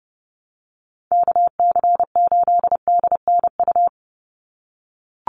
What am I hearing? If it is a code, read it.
Morse code, heard as a keyed tone, about 30 wpm, 700 Hz. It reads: KC8DNU
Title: USA Call Signs 30wpm